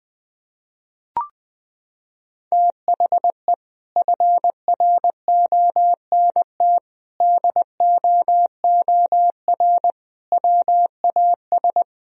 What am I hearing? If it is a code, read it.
Morse code, heard as a keyed tone, about 20 wpm, 700 Hz. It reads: THE FRONT DOOR WAS
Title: Sentences from Top 1000 Words - Review Part 2 - 20wpm